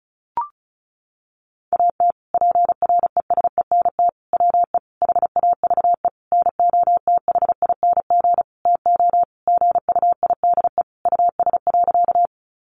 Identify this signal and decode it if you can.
Morse code, heard as a keyed tone, about 34 wpm, 700 Hz. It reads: AT PRESENT WE HAVE NOTHING TO GUIDE US.
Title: Sentences from Top 800 Words 35wpm